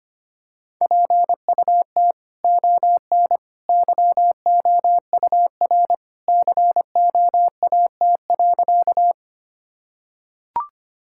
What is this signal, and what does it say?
Morse code, heard as a keyed tone, about 25 wpm, 700 Hz. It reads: PUT ON YOUR COAT.
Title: Sentences from Top 1600 Words 25wpm